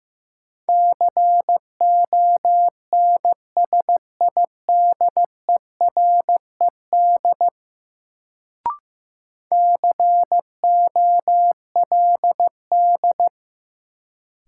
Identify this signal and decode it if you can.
Morse code, heard as a keyed tone, about 15 wpm, 700 Hz. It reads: CONSIDERED COLD
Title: Sets of 5 Words - New Word Drill - 15wpm